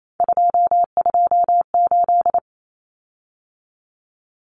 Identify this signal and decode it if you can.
Morse code, heard as a keyed tone, about 28 wpm, 700 Hz. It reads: 228
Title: Common Cut Numbers - 3-Digit 28wpm